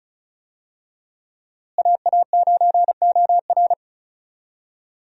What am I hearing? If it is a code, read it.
Morse code, heard as a keyed tone, about 36 wpm, 700 Hz. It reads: AA9OR
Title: USA Call Signs - 5 Characters - Repeated 3x - 35wpm